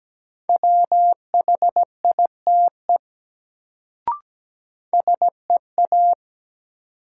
Morse code, heard as a keyed tone, about 17 wpm, 700 Hz. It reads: WHITE SEA
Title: Sentences from Top 300 Words - New Word Drill - 17wpm